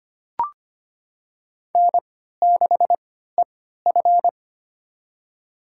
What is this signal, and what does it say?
Morse code, heard as a keyed tone, about 25 wpm, spaced as for 14 wpm, 700 Hz. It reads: N6EF
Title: USA Call Signs - 3x Character Spacing - 25wpm